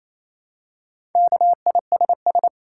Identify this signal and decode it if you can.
Morse code, heard as a keyed tone, about 28 wpm, 700 Hz. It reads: KISS
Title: Four Letter Words - 28wpm